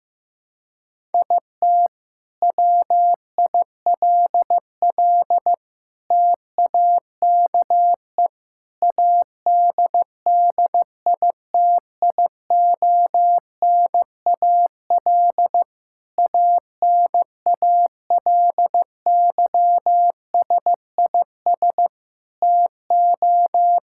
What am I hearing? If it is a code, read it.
Morse code, heard as a keyed tone, about 15 wpm, 700 Hz. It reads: IT WILL TAKE ADDITIONAL ANALYSIS TO
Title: Sentences from Top 1000 Words - Review Part 5 - 15wpm